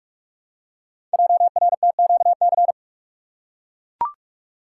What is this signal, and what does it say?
Morse code, heard as a keyed tone, about 44 wpm, 700 Hz. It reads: JRTQC
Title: Code Groups - 5 Letters - 45wpm